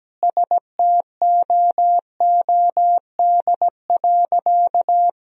Morse code, heard as a keyed tone, about 17 wpm, 700 Hz. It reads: STOOD.
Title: 2x Word-Spacing - Sentences from Top 500 Words - 17wpm